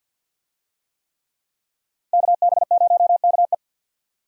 Morse code, heard as a keyed tone, about 51 wpm, 700 Hz. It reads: KB0KE
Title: USA Call Signs - 5 Characters - 50wpm